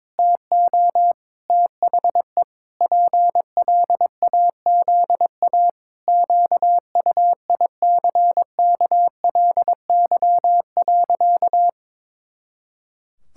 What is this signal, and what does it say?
Morse code, heard as a keyed tone, about 22 wpm, 700 Hz. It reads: TO THE PLAZA QUICKLY.
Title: Pangram Challenge v2 - 22wpm